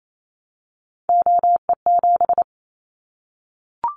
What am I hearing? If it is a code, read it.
Morse code, heard as a keyed tone, about 28 wpm, 700 Hz. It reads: OE7
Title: Callsign Prefixes for Top 50 DXCC - 28wpm